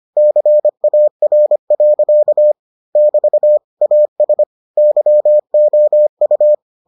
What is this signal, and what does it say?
Morse code, heard as a keyed tone, about 25 wpm, 600 Hz. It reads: CAR. <BT> AS YOU